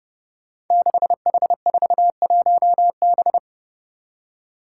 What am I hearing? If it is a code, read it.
Morse code, heard as a keyed tone, about 28 wpm, 700 Hz. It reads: 6H41B
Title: Five Challenging Letter-Number Combinations 30wpm